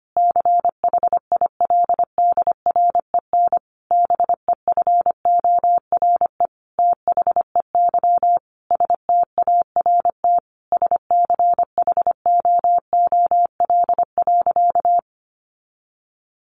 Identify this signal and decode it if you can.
Morse code, heard as a keyed tone, about 25 wpm, 700 Hz. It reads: CHILDREN BEFORE THEY START SCHOOL.
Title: Sentences from Top 1100 Words 25wpm